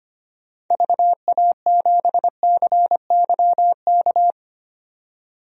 Morse code, heard as a keyed tone, about 25 wpm, 700 Hz. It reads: VA7CYK
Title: DXCC-100 Call Signs - 6 Characters - 25wpm